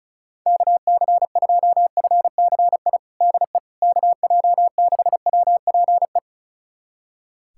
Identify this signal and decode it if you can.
Morse code, heard as a keyed tone, about 36 wpm, 700 Hz. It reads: KC2FCI DE KJ6WPE
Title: USA Callsign DE Callsign 35wpm